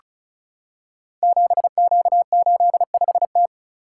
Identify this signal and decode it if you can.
Morse code, heard as a keyed tone, about 36 wpm, 700 Hz. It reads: 7Q85T
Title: Code Groups - 5 Letter:Numbers - 35wpm